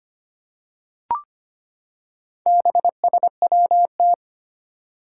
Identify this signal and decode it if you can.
Morse code, heard as a keyed tone, about 25 wpm, 700 Hz. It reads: BSWT
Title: Four Challenging Letter-Number Combinations 25wpm